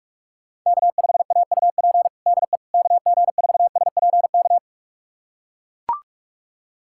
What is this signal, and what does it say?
Morse code, heard as a keyed tone, about 42 wpm, 700 Hz. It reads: K5AUP DE KG4SPK
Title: USA Callsign DE Callsign 45wpm